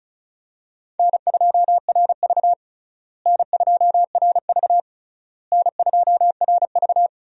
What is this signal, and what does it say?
Morse code, heard as a keyed tone, about 36 wpm, 700 Hz. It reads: N2RV N2RV N2RV
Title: USA Call Signs - 4 Characters - Repeated 3x - 35wpm